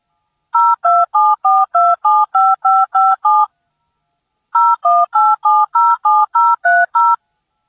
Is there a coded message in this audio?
Yes: DTMF from 0.5 to 7.2 s, heard as telephone touch tones, tones of about 220 ms. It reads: 0274275557 018707030